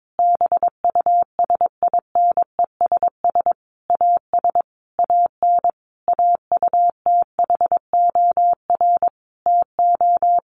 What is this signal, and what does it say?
Morse code, heard as a keyed tone, about 22 wpm, 700 Hz. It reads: BUSINESS AS AN AUTHOR TO
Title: Sentences from Top 1200 Words 22wpm